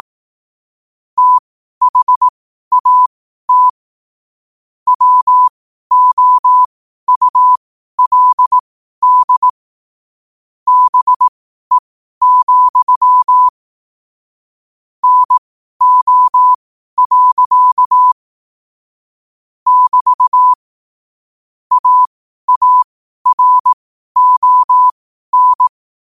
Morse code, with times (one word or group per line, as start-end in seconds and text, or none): 1.2-3.7 THAT
4.9-9.5 WOULD
10.7-13.5 BE,
15.0-18.1 NO.
19.7-20.5 <BT>
21.7-25.7 AARON